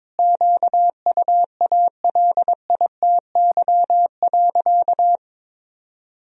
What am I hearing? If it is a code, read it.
Morse code, heard as a keyed tone, about 22 wpm, 700 Hz. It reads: QUALITY.
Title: Sentences from Top 1000 Words - Review Part 4 - 22wpm